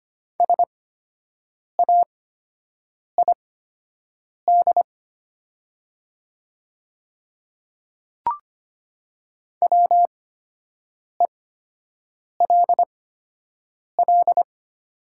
Morse code, heard as a keyed tone, about 25 wpm, spaced as for 7 wpm, 700 Hz. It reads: SAID WELL
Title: 8x Character Spacing - Top 200 Words - 25wpm